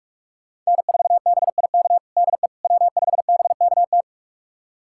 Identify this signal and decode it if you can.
Morse code, heard as a keyed tone, about 42 wpm, 700 Hz. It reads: N4BIK DE W5BKT
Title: USA Callsign DE Callsign - 5 Characters - 45wpm